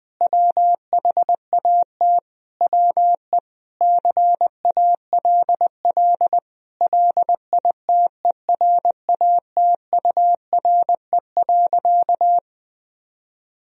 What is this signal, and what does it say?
Morse code, heard as a keyed tone, about 21 wpm, 700 Hz. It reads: WHAT WE CALL LITERATURE.
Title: Sentences from Top 1000 Words - Review Part 3 - 20wpm